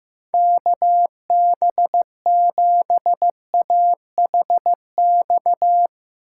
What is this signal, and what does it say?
Morse code, heard as a keyed tone, about 15 wpm, 700 Hz. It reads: KB7AHX
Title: USA Call Signs - 6x Character Spacing - 15wpm